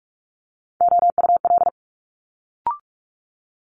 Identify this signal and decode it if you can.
Morse code, heard as a keyed tone, about 46 wpm, 700 Hz. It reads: OUL
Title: Multi-letter Phonemes Quick Fire 45wpm